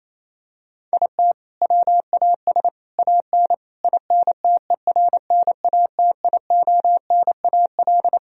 Morse code, heard as a keyed tone, about 27 wpm, 700 Hz. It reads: IT WAS AN INTERNATIONAL
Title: Sentences from Top 1000 Words - Review Part 4 - 28wpm